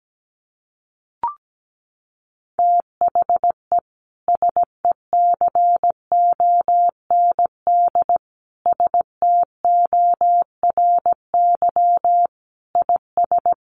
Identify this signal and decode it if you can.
Morse code, heard as a keyed tone, about 17 wpm, 700 Hz. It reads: THE SECOND STORY IS
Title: Sentences from Top 1400 Words 17wpm